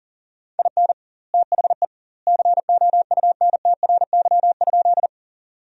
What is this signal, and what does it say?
Morse code, heard as a keyed tone, about 40 wpm, 700 Hz. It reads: IN THE COUNTRY?
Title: 2x Word-Spacing - Sentences from Top 300 Words - 40wpm